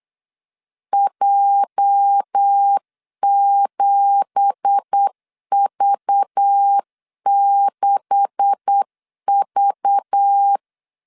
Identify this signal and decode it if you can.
Morse code, heard as a keyed tone, about 8 wpm, 800 Hz. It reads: J7V6V
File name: CW / DET / 1 9 1500C S A 8wpm 5637